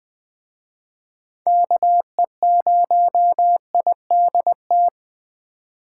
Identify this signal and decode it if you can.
Morse code, heard as a keyed tone, about 21 wpm, 700 Hz. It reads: KE0IDT
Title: Speed Racing - USA 2x3 Call Signs 17wpm